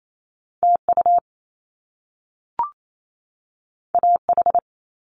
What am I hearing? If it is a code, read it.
Morse code, heard as a keyed tone, about 28 wpm, 700 Hz. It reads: T U A H
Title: Top Two Letter Combinations in Words 28wpm